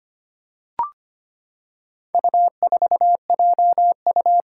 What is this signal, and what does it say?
Morse code, heard as a keyed tone, about 25 wpm, 700 Hz. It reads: U4JU
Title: Four Challenging Letter-Number Combinations 25wpm